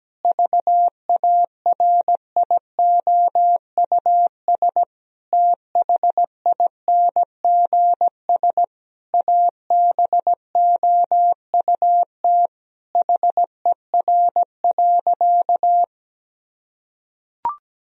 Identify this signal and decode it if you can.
Morse code, heard as a keyed tone, about 17 wpm, 700 Hz. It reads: VARIOUS THINGS ABOUT HER.
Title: Sentences from Top 1000 Words - Review Part 4 - 17wpm